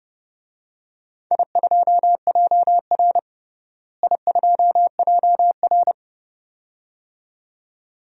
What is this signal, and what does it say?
Morse code, heard as a keyed tone, about 30 wpm, 700 Hz. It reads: I2JR I2JR
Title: DXCC-100 Call Signs - 4 Characters - Repeated 2x - 30wpm